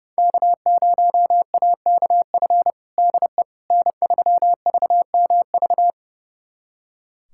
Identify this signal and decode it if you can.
Morse code, heard as a keyed tone, about 32 wpm, 700 Hz. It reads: K0AKF DE N3VMV
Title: USA Callsign DE Callsign - 5 Characters - 30wpm